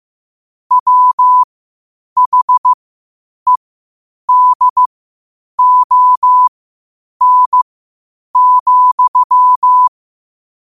Morse code, heard as a keyed tone, about 15 wpm, spaced as for 8 wpm, 1000 Hz. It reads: WHEDON,
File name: qotdfiles / 8wpm